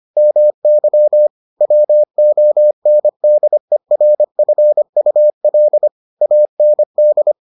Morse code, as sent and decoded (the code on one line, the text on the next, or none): -- -.-- / .-- --- -. -.. .-. ..-. ..- .-.. / .- -. -..
MY WONDERFUL AND